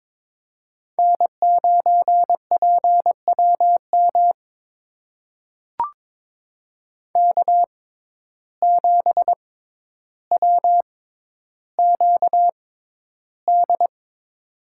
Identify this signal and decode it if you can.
Morse code, heard as a keyed tone, about 22 wpm, spaced as for 11 wpm, 700 Hz. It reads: N9PWM K7WQD
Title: USA Call Signs - 5 Characters - 6x Character Spacing - 22wpm